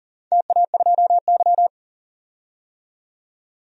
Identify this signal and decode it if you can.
Morse code, heard as a keyed tone, about 40 wpm, 700 Hz. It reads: TA2Y
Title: DXCC-100 Call Signs - 4 Characters - 40wpm